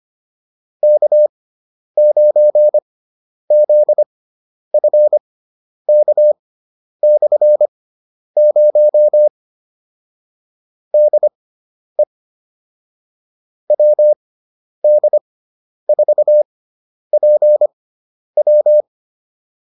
Morse code, heard as a keyed tone, about 25 wpm, spaced as for 10 wpm, 600 Hz. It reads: K9ZFK/0 DE WD4PW
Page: intermediate-practice-files